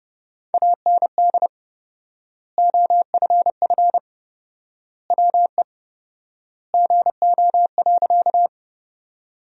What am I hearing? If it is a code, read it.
Morse code, heard as a keyed tone, about 30 wpm, 700 Hz. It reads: AND OFF WE GO.
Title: 4x Word-Spacing - Sentences from Top 200 Words - 30wpm